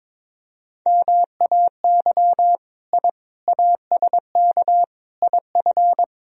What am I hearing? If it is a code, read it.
Morse code, heard as a keyed tone, about 22 wpm, 700 Hz. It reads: MAY I ASK IF
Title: Sentences from Top 1000 Words - Review Part 3 - 22wpm